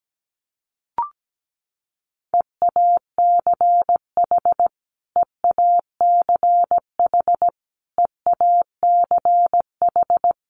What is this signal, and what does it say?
Morse code, heard as a keyed tone, about 17 wpm, 700 Hz. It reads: EACH EACH EACH
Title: Top 200 Words Repeated 17wpm